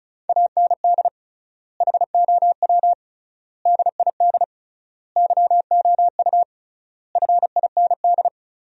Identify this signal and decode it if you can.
Morse code, heard as a keyed tone, about 36 wpm, 700 Hz. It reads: AND HOW DID YOU FIND
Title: 3x Word-Spacing - Sentences from Top 100 Words - 35wpm